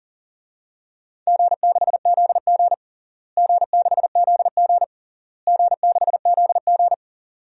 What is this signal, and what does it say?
Morse code, heard as a keyed tone, about 40 wpm, 700 Hz. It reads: G6ZG G6ZG G6ZG
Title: DXCC-100 Call Signs - 4 Characters - Repeated 3x - 40wpm